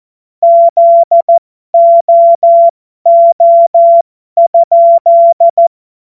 Morse code, heard as a keyed tone, about 14 wpm, spaced as for 12 wpm, 680 Hz. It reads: ZOO?